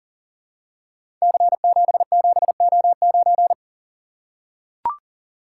Morse code, heard as a keyed tone, about 43 wpm, 700 Hz. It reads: C77O9
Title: Code Groups - 5 Letter:Numbers - 40wpm